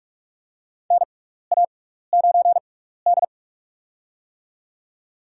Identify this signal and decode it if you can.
Morse code, heard as a keyed tone, about 44 wpm, spaced as for 15 wpm, 700 Hz. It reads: NA9D
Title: USA Call Signs - 4 Characters - 6x Character Spacing - 45wpm